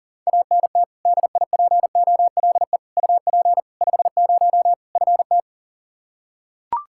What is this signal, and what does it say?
Morse code, heard as a keyed tone, about 40 wpm, 700 Hz. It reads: ANT DIPOLE UP 50 FT
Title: Practice QSO - Exchange 2 - Info - 40wpm